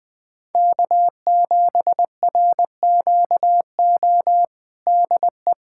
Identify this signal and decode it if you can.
Morse code, heard as a keyed tone, about 20 wpm, 700 Hz. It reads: K7RQO DE
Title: USA Callsign DE Callsign - 5 Characters - 20wpm